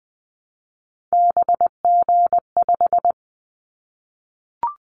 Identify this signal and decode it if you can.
Morse code, heard as a keyed tone, about 20 wpm, 700 Hz. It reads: BG5
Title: Callsign Prefixes for Top 100 DXCC - 20wpm